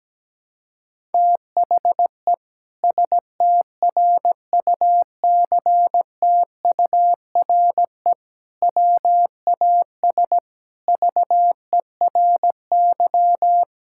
Morse code, heard as a keyed tone, about 17 wpm, 700 Hz. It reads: THE STRUCTURE WAS VERY
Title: Sentences from Top 1000 Words - Review Part 6 - 17wpm